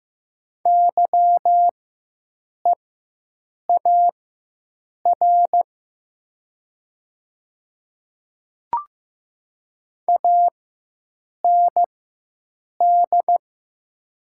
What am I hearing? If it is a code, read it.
Morse code, heard as a keyed tone, about 15 wpm, spaced as for 7 wpm, 700 Hz. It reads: YEAR AND
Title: 4x Character Spacing - Top 200 Words - 15wpm